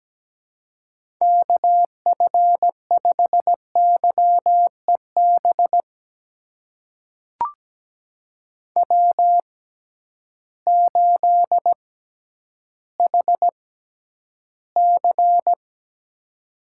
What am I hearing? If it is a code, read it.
Morse code, heard as a keyed tone, about 17 wpm, spaced as for 10 wpm, 700 Hz. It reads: KF5YEB W8HC
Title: USA Call Signs - 6x Character Spacing - 17wpm